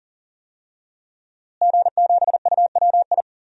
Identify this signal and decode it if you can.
Morse code, heard as a keyed tone, about 40 wpm, 700 Hz. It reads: G7UWI
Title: DXCC-100 Call Signs - 5 Characters - Repeated 3x - 40wpm